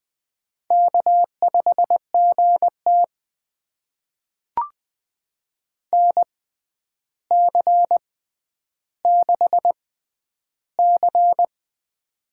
Morse code, heard as a keyed tone, about 20 wpm, spaced as for 10 wpm, 700 Hz. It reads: K5GT NC6C
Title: USA Call Signs - 6x Character Spacing - 20wpm